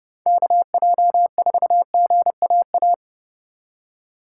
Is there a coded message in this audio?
Yes: Morse code from 0.3 to 2.9 s, heard as a keyed tone, about 40 wpm, 700 Hz. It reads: KJ4GAA